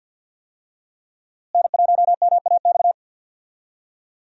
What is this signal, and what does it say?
Morse code, heard as a keyed tone, about 48 wpm, 690 Hz. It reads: N1MAX
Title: USA Call Signs - 5 Characters - Repeated 3x - 50wpm